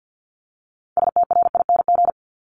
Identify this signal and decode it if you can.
Morse code, heard as a keyed tone, about 50 wpm, 690 Hz. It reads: STRING
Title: Multi-letter Phonemes Repeated 3x 50wpm